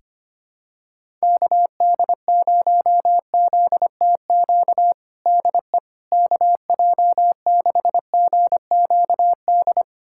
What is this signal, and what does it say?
Morse code, heard as a keyed tone, about 25 wpm, 700 Hz. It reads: KD0ZTQ DE KJ6GQD
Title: USA Callsign DE Callsign 25wpm